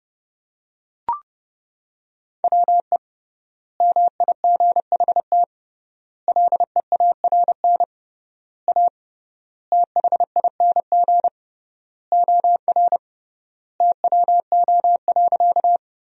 Morse code, heard as a keyed tone, about 30 wpm, 700 Hz. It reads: WE MIGHT LEARN A THING OR TWO.
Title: 3x Word-Spacing - Sentences from Top 300 Words - 30wpm